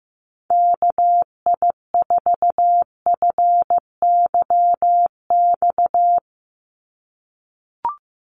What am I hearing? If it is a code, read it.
Morse code, heard as a keyed tone, about 15 wpm, 700 Hz. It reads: KI4FYX
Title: USA Call Signs 15wpm